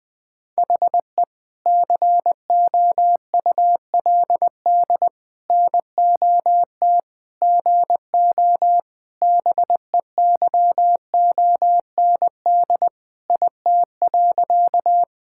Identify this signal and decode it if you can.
Morse code, heard as a keyed tone, about 20 wpm, 700 Hz. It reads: HE COULD NOT GO BEYOND IT.
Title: Sentences from Top 1000 Words - Review Part 6 - 20wpm